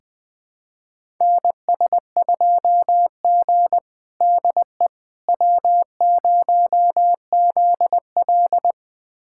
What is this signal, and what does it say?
Morse code, heard as a keyed tone, about 20 wpm, 700 Hz. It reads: NS2G DE W0ZL
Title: USA Callsign DE Callsign - 4 Characters - 20wpm